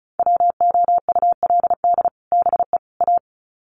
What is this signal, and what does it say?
Morse code, heard as a keyed tone, about 34 wpm, 700 Hz. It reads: WOULD BE A
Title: Sets of 3 Words 35wpm